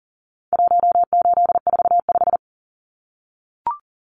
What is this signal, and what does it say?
Morse code, heard as a keyed tone, about 40 wpm, 700 Hz. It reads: 1845
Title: Numbers - 4-Digit 40wpm